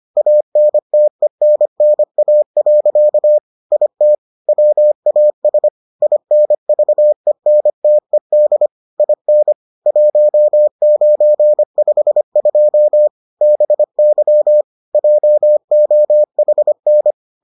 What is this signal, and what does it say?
Morse code, heard as a keyed tone, about 25 wpm, 600 Hz. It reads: ANTENNA. IT WAS INVENTED IN 1952 BY JOHN